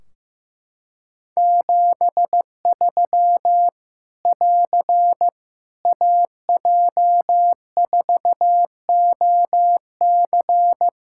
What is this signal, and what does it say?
Morse code, heard as a keyed tone, about 15 wpm, 700 Hz. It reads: 73 <AR> AJ4OC
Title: Practice QSO - Exchange 4 - Ending - 15wpm